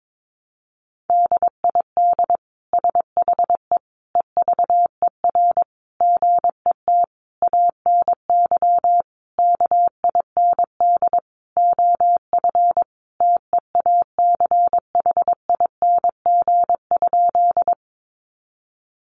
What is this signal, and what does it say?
Morse code, heard as a keyed tone, about 22 wpm, 700 Hz. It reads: DID SHE EVER GET ANY KIND OF TEACHING?
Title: Sentences from Top 1100 Words 22wpm